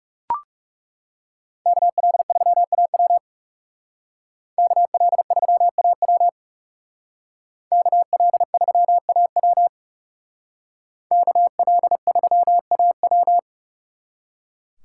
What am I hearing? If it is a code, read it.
Morse code, heard as a keyed tone, about 36 wpm, 700 Hz. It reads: KL3AW KL3AW KL3AW KL3AW
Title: Speed Racing - USA 2x2 Call Signs 30wpm